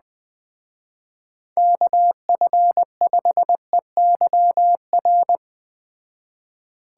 Morse code, heard as a keyed tone, about 20 wpm, 700 Hz. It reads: KF5EYR